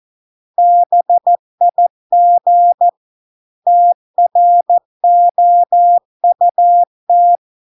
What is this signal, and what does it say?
Morse code, heard as a keyed tone, about 14 wpm, 700 Hz. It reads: BIG TROUT